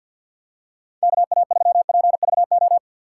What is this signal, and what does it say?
Morse code, heard as a keyed tone, about 46 wpm, 700 Hz. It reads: KA3PVO
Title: USA Call Signs - 6 Characters - 6x Character Spacing - 50wpm